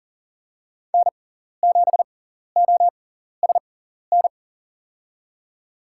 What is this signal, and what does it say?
Morse code, heard as a keyed tone, about 38 wpm, spaced as for 14 wpm, 690 Hz. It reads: N7OSN
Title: USA Call Signs - 5 Characters - 6x Character Spacing - 40wpm